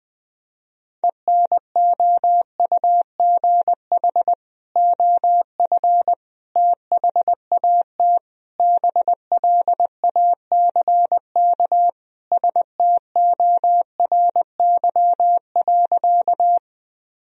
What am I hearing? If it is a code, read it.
Morse code, heard as a keyed tone, about 21 wpm, 700 Hz. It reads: ENOUGH OF THAT BLACK STORY.
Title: Sentences from Top 1000 Words - Review Part 2 - 20wpm